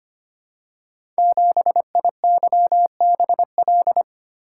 Morse code, heard as a keyed tone, about 25 wpm, 700 Hz. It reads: 7IYBL
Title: Code Groups - 5 Letter:Numbers - 25wpm